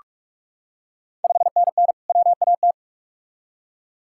Morse code, heard as a keyed tone, about 42 wpm, 690 Hz. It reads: 5NN WAT
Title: Ontario QSO Party Exchange 45wpm